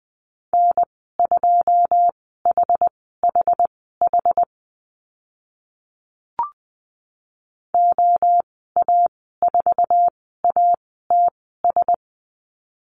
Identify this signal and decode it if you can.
Morse code, heard as a keyed tone, about 20 wpm, spaced as for 14 wpm, 700 Hz. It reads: N2HHH OA4ATS
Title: 088 Lesson 15g - Number 5 - Call Signs 20wpm